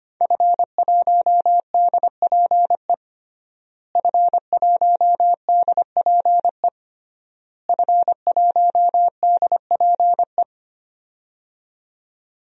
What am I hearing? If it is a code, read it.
Morse code, heard as a keyed tone, about 25 wpm, 700 Hz. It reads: F1DPE F1DPE F1DPE
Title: DXCC-100 Call Signs - 5 Characters - Repeated 3x - 25wpm